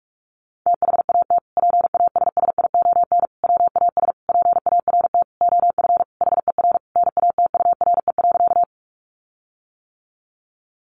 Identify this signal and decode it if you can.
Morse code, heard as a keyed tone, about 44 wpm, 700 Hz. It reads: THAT PASSION WAS PART OF HER NATURE.
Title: Sentences from Top 1300 Words 45wpm